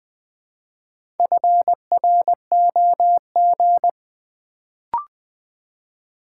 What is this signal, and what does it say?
Morse code, heard as a keyed tone, about 20 wpm, 700 Hz. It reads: FROG
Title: Build A Word - 2 Letters At A Time - 4-Letter Words - 20wpm